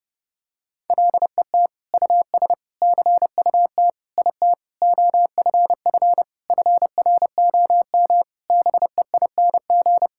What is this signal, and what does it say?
Morse code, heard as a keyed tone, about 30 wpm, 700 Hz. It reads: LET US CUT IT OFF FROM BEING
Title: Sentences from Top 1000 Words - Review Part 6 - 30wpm